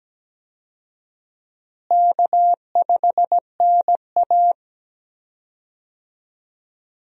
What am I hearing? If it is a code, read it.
Morse code, heard as a keyed tone, about 17 wpm, 700 Hz. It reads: K5NA
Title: USA Call Signs - 4 Characters - Repeated 3x - 17wpm